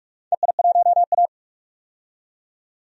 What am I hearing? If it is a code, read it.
Morse code, heard as a keyed tone, about 45 wpm, 700 Hz. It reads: EI1A
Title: DXCC-100 Call Signs - 4 Characters - 45wpm